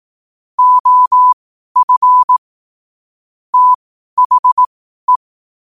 Morse code, heard as a keyed tone, about 18 wpm, spaced as for 12 wpm, 1000 Hz. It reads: OF THE
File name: qotdfiles / 12wpm